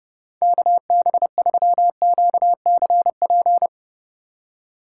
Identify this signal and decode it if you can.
Morse code, heard as a keyed tone, about 37 wpm, 700 Hz. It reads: KB3QCP
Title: Speed Racing - USA 2x3 Call Signs 30wpm